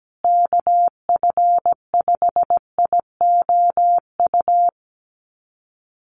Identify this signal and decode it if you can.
Morse code, heard as a keyed tone, about 17 wpm, 700 Hz. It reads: KF5IOU
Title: USA Call Signs Repeated 2x 17wpm